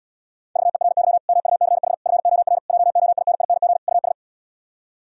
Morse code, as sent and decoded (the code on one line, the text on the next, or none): ...- . .-. -.-- / -- ..- -.-. .... / ..-. --- .-. / .--- --- .. -. .. -. --. / ..- ...
VERY MUCH FOR JOINING US